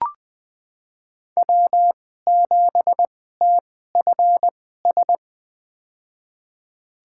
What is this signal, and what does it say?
Morse code, heard as a keyed tone, about 20 wpm, spaced as for 14 wpm, 700 Hz. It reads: W7TFS
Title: USA Call Signs - 5 Characters - 2x Character Spacing - 20wpm